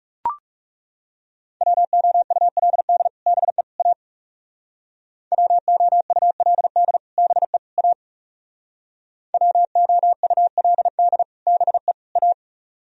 Morse code, heard as a keyed tone, about 36 wpm, 700 Hz. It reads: WOULD BE A WOULD BE A WOULD BE A
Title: Speed-Racing - Sets of 3 Words 30wpm